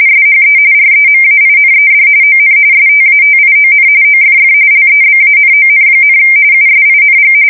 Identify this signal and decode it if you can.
RTTY, about 45.45 baud. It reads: SHE REMAINED ON STATION OFF LEBANON, USING
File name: fsk170 / USS-New-Jersey